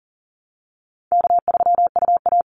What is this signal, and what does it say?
Morse code, heard as a keyed tone, about 40 wpm, 700 Hz. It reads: K3UA
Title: Easy Call Signs Repeated 2x - 40wpm